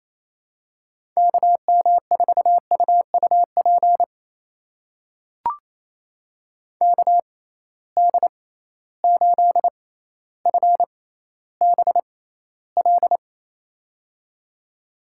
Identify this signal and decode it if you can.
Morse code, heard as a keyed tone, about 28 wpm, spaced as for 14 wpm, 700 Hz. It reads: KM4UUP KD8FBL
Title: USA Call Signs - 6 Characters - 6x Character Spacing - 28wpm